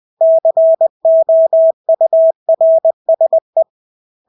Morse code, heard as a keyed tone, about 20 wpm, 650 Hz. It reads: COURSE
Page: Engelske ord 20 wpm